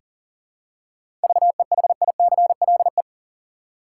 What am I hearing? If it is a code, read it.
Morse code, heard as a keyed tone, about 39 wpm, 700 Hz. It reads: VEHICLE
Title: Seven Letter Words - 40wpm